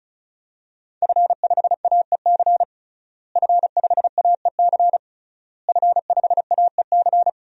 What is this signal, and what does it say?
Morse code, heard as a keyed tone, about 36 wpm, 700 Hz. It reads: F5AEC F5AEC F5AEC
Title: DXCC-100 Call Signs - 5 Characters - Repeated 3x - 35wpm